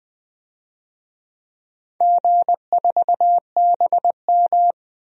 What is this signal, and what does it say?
Morse code, heard as a keyed tone, about 20 wpm, 700 Hz. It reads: G4BM
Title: DXCC-100 Call Signs - 4 Characters - 20wpm